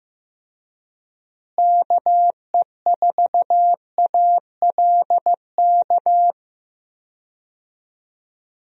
Morse code, heard as a keyed tone, about 15 wpm, 700 Hz. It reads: KE4ALK
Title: USA Call Signs - 6 Characters - Repeated 3x - 15wpm